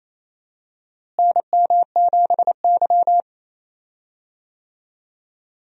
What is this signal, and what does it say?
Morse code, heard as a keyed tone, about 28 wpm, 700 Hz. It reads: NM7Y
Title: USA Call Signs - 4 Characters - Repeated 2x - 28wpm